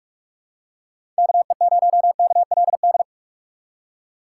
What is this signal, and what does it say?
Morse code, heard as a keyed tone, about 42 wpm, 700 Hz. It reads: KE0KLD
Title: USA Call Signs - 6 Characters - 6x Character Spacing - 45wpm